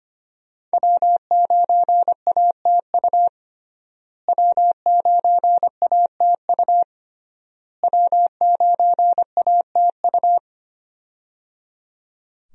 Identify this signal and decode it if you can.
Morse code, heard as a keyed tone, about 25 wpm, 700 Hz. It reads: W9ATU W9ATU W9ATU
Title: USA Call Signs - 5 Characters - Repeated 3x - 25wpm